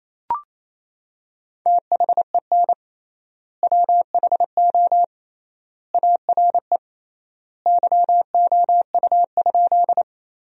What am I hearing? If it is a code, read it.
Morse code, heard as a keyed tone, about 27 wpm, 700 Hz. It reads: THEN WHO ARE YOU?
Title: 3x Word-Spacing - Sentences from Top 100 Words - 28wpm